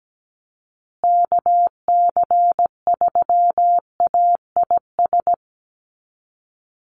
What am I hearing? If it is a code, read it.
Morse code, heard as a keyed tone, about 17 wpm, 700 Hz. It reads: KC3AIS
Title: USA Call Signs Repeated 3x 17wpm